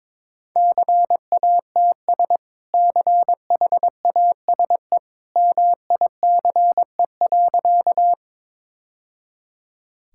Morse code, heard as a keyed tone, about 22 wpm, 700 Hz. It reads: CATS CHASE MICE.
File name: practice / 22wpm / Easy Sentences with Hints - 22wpm